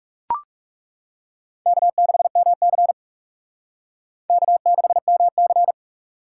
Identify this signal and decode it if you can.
Morse code, heard as a keyed tone, about 38 wpm, 700 Hz. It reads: K6MC K6MC